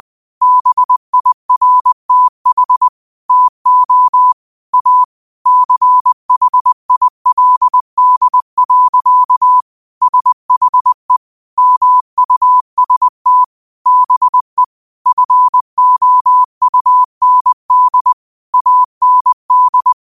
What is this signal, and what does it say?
Morse code, heard as a keyed tone, about 20 wpm, 1000 Hz. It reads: BIRTH TO A CHILD. SHE MUST BE FOUND AND